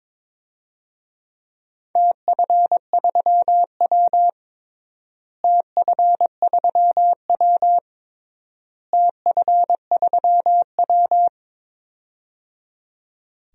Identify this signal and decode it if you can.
Morse code, heard as a keyed tone, about 22 wpm, 700 Hz. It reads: TF3W TF3W TF3W
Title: DXCC-100 Call Signs - 4 Characters - Repeated 3x - 22wpm